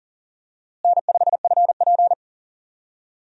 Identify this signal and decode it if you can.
Morse code, heard as a keyed tone, about 40 wpm, 700 Hz. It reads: N5FP
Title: USA Call Signs - 4 Characters - Repeated 2x - 40wpm